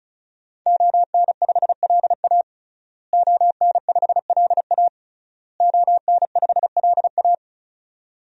Morse code, heard as a keyed tone, about 34 wpm, 700 Hz. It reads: ON5LA ON5LA ON5LA
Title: DXCC-100 Call Signs - 5 Characters - Repeated 3x - 35wpm